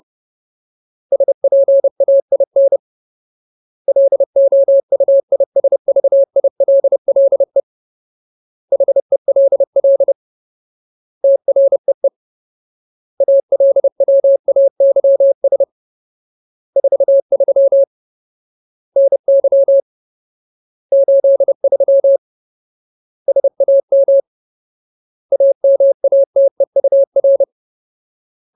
Morse code, heard as a keyed tone, about 30 wpm, 550 Hz. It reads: SPAIN LOUISVILLE HELL TREE ALWAYS 43 NY 83 SAM AMATEUR
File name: Ultimate Word List / 30WPM Ultimate